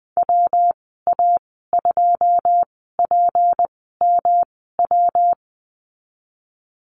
Morse code, heard as a keyed tone, about 19 wpm, spaced as for 14 wpm, 700 Hz. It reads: WA2PMW
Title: 173 Lesson 28g - Number 9 - Call Signs 20wpm